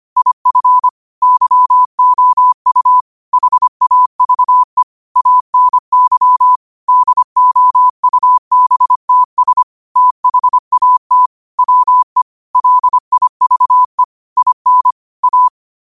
Morse code, heard as a keyed tone, about 25 wpm, 1000 Hz. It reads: IF YOU HAVE ANY DOUBTS THAT WE LIVE IN A
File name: qotdfiles / 25wpm